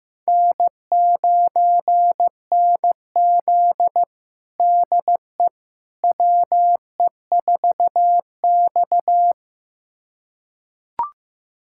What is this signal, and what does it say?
Morse code, heard as a keyed tone, about 15 wpm, 700 Hz. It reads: N9NZ DE WE4X
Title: USA Callsign DE Callsign - 4 Characters - 15wpm